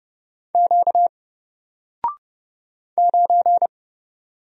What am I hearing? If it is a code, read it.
Morse code, heard as a keyed tone, about 31 wpm, 700 Hz. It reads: Q 9
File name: Single-Letter-Number-30wpm